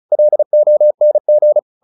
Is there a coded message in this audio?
Yes: Morse code from 0.1 to 1.6 s, heard as a keyed tone, about 34 wpm, 600 Hz. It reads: LONG